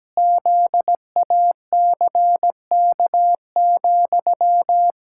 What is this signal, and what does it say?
Morse code, heard as a keyed tone, about 17 wpm, 700 Hz. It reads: ZACK,
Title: Pangram Challenge v2 - 17wpm